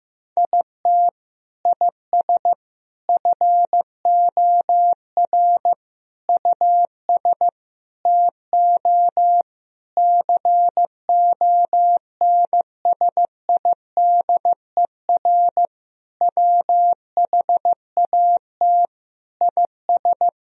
Morse code, heard as a keyed tone, about 15 wpm, 700 Hz. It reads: IT IS FOR US TO CONSIDER WHAT IS
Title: Sentences from Top 1000 Words - Review Part 4 - 15wpm